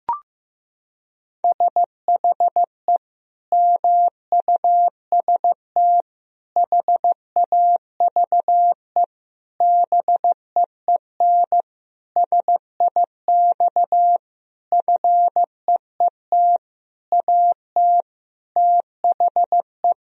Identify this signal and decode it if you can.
Morse code, heard as a keyed tone, about 15 wpm, 700 Hz. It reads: SHE MUST HAVE BEEN SIX FEET AT THE
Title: Sentences from Top 1000 Words - Review Part 2 - 15wpm